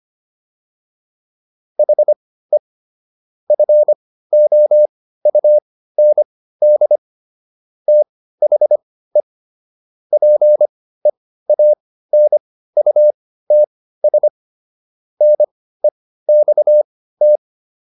Morse code, heard as a keyed tone, about 25 wpm, spaced as for 15 wpm, 600 Hz. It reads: HE FOUND THE PEANUTS NEXT